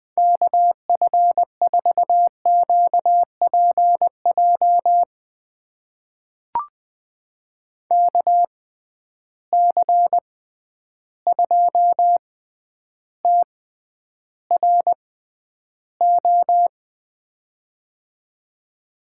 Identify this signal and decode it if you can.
Morse code, heard as a keyed tone, about 20 wpm, spaced as for 10 wpm, 700 Hz. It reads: KF4QPJ KC2TRO
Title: USA Call Signs - 6 Characters - 6x Character Spacing - 20wpm